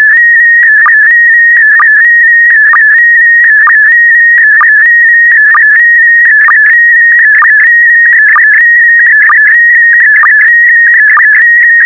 SSTV (slow-scan television), PD 290.